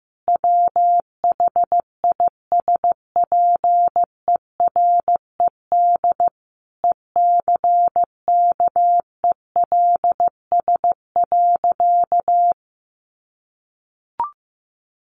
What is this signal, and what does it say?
Morse code, heard as a keyed tone, about 15 wpm, 700 Hz. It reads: WHISPERED ECKELS.
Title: A Sound Of Thunder 15wpm